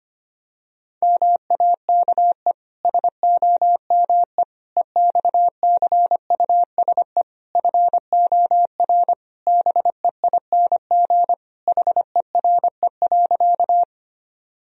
Morse code, heard as a keyed tone, about 25 wpm, 700 Hz. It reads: MAKE SOME EXCUSE FOR BEING HERE.
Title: Sentences from Top 1000 Words - Review Part 3 - 25wpm